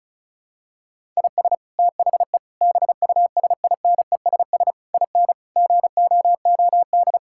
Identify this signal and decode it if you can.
Morse code, heard as a keyed tone, about 36 wpm, 700 Hz. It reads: IS THE BUSINESS IN GOOD
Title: Sentences from Top 1000 Words - Review Part 5 - 35wpm